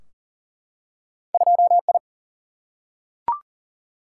Morse code, heard as a keyed tone, about 40 wpm, 700 Hz. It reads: 2I
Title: Single-Letter Single-Number 40wpm